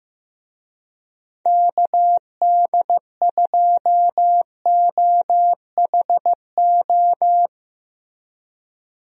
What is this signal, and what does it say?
Morse code, heard as a keyed tone, about 15 wpm, 700 Hz. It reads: KD2OHO
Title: USA Call Signs - 6 Characters - Repeated 2x - 15wpm